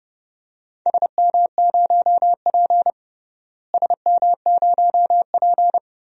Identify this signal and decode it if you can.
Morse code, heard as a keyed tone, about 32 wpm, 700 Hz. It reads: SM0P SM0P
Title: DXCC-100 Call Signs - 4 Characters - Repeated 2x - 30wpm